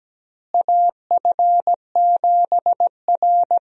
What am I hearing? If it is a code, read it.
Morse code, heard as a keyed tone, about 17 wpm, 700 Hz. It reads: AF7R
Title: USA Call Signs - 4 Characters - Repeated 2x - 17wpm